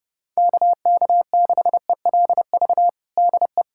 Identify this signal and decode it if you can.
Morse code, heard as a keyed tone, about 31 wpm, 700 Hz. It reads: KK6ELV DE
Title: USA Callsign DE Callsign - 6 Characters - 30wpm